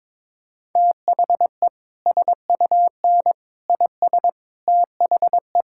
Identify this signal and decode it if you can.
Morse code, heard as a keyed tone, about 22 wpm, 700 Hz. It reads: THE SUN IS THE
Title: Sentences from Top 1000 Words - Review Part 5 - 22wpm